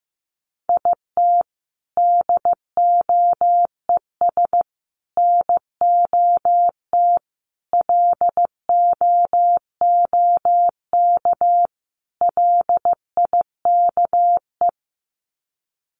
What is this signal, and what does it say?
Morse code, heard as a keyed tone, about 15 wpm, 700 Hz. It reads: IT DOES NOT LOOK LIKE
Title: Sets of 5 Words Encore2 15wpm